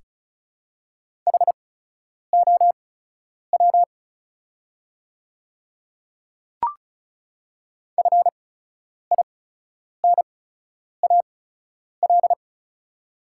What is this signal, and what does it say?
Morse code, heard as a keyed tone, about 36 wpm, spaced as for 10 wpm, 690 Hz. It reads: HOW FINAL
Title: 8x Character Spacing - Top 500 Words - 35wpm